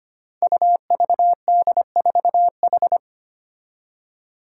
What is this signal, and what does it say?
Morse code, heard as a keyed tone, about 25 wpm, 700 Hz. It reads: UVD4H
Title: Five Challenging Letter-Number Combinations 25wpm